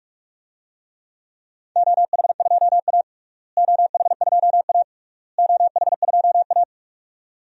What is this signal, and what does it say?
Morse code, heard as a keyed tone, about 41 wpm, 700 Hz. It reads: OH2A OH2A OH2A
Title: DXCC-100 Call Signs - 4 Characters - Repeated 3x - 45wpm